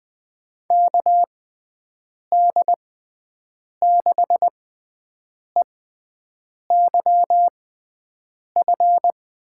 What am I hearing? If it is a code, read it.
Morse code, heard as a keyed tone, about 20 wpm, spaced as for 7 wpm, 700 Hz. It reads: KD6EYF
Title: USA Call Signs - 6x Character Spacing - 20wpm